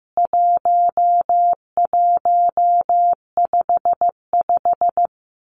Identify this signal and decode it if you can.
Morse code, heard as a keyed tone, about 14 wpm, 700 Hz. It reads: 1155
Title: CWT Exchange 15wpm